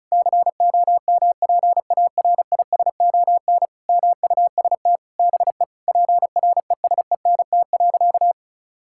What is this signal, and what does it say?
Morse code, heard as a keyed tone, about 36 wpm, 700 Hz. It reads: COMPARISON MUST BE PRESENT.
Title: Sentences from Top 1600 Words 35wpm